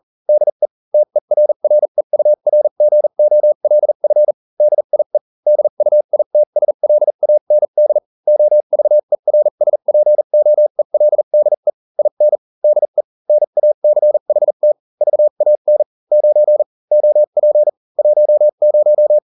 Morse code, heard as a keyed tone, about 40 wpm, spaced as for 35 wpm, 600 Hz. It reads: DE TERREURGOLF DIE DUITSLAND OVERSPOELDE IN DE NACHT VAN 9 OP 10